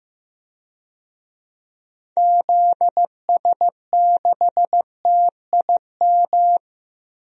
Morse code, heard as a keyed tone, about 15 wpm, 700 Hz. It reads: ZS6TIM
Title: DXCC-100 Call Signs - 6 Characters - Repeated 2x - 15wpm